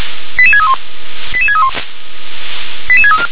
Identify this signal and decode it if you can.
Five-tone selective call (ZVEI), heard as quick runs of five tones, tones of about 70 ms. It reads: 88631 88631 88634